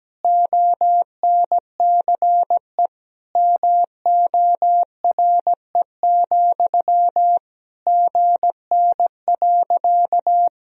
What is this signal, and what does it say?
Morse code, heard as a keyed tone, about 17 wpm, 700 Hz. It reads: ONCE MORE, GN.
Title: Sentences from Top 300 Words and Abbreviations 17wpm